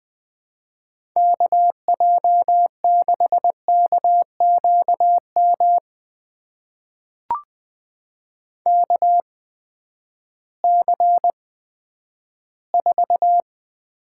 Morse code, heard as a keyed tone, about 20 wpm, spaced as for 11 wpm, 700 Hz. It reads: KJ6KQM KC4
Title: USA Call Signs - 6 Characters - 8x Character Spacing - 20wpm